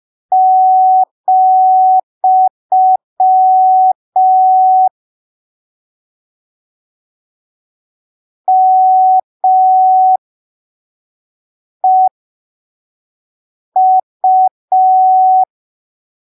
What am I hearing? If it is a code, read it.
Morse code, heard as a keyed tone, about 5 wpm, spaced as for 3 wpm, 750 Hz. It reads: , MEU
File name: arquivos / texto / texto 26 05ppm